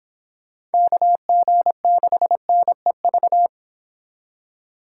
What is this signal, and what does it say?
Morse code, heard as a keyed tone, about 29 wpm, 700 Hz. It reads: KG6NEV